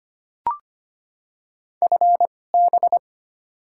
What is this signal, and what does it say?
Morse code, heard as a keyed tone, about 25 wpm, spaced as for 18 wpm, 700 Hz. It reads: FB
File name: Top 20 QSO Elements - 2x Character Spacing - 25wpm